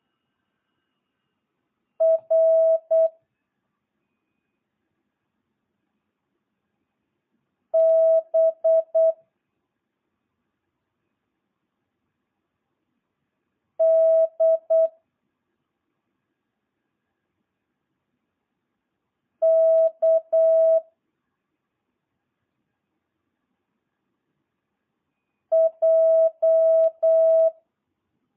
Morse code, heard as a keyed tone, about 8 wpm, 650 Hz. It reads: R B D K J